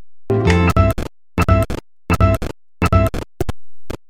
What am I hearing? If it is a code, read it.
There is no code.